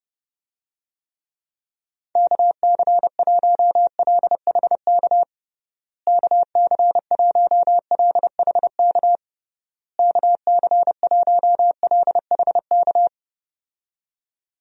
Morse code, heard as a keyed tone, about 30 wpm, 700 Hz. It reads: KC1LHK KC1LHK KC1LHK
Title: USA Call Signs - 6 Characters - Repeated 3x - 30wpm